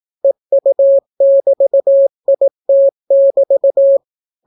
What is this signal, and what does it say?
Morse code, heard as a keyed tone, about 18 wpm, 550 Hz. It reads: EU<BT>IT<BT>